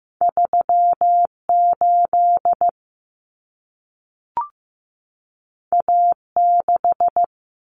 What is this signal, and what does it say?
Morse code, heard as a keyed tone, about 15 wpm, 700 Hz. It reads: 38 A6
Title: Common Cut Numbers - 2-Digit 15wpm